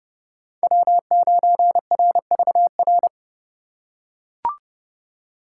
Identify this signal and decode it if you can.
Morse code, heard as a keyed tone, about 31 wpm, 700 Hz. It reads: W9RVR
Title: USA Call Signs - 5 Characters - 4x Character Spacing - 30wpm